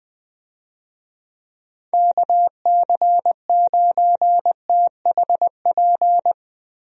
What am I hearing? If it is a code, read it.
Morse code, heard as a keyed tone, about 20 wpm, 700 Hz. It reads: KC9THP
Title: USA Call Signs - 6 Characters - Repeated 2x - 20wpm